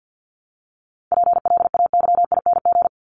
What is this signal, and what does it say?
Morse code, heard as a keyed tone, about 49 wpm, 700 Hz. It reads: PLAYING
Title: ING Suffix - Top 100 Words with Base Words 50wpm